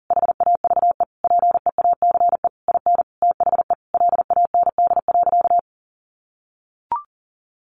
Morse code, heard as a keyed tone, about 40 wpm, 700 Hz. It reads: HAVE PEACE IN THE LAND.
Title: Sentences from Top 1500 Words 40wpm